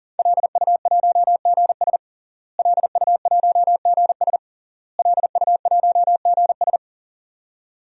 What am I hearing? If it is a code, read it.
Morse code, heard as a keyed tone, about 40 wpm, 700 Hz. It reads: LU1GS LU1GS LU1GS
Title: DXCC-100 Call Signs - 5 Characters - Repeated 3x - 40wpm